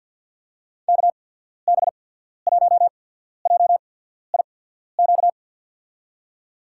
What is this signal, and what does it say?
Morse code, heard as a keyed tone, about 51 wpm, spaced as for 14 wpm, 700 Hz. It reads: KB1JIQ
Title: USA Call Signs - 6 Characters - 8x Character Spacing - 50wpm